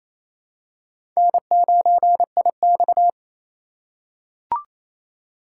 Morse code, heard as a keyed tone, about 29 wpm, 690 Hz. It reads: N9IX
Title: USA Call Signs - 4 Characters - 2x Character Spacing - 28wpm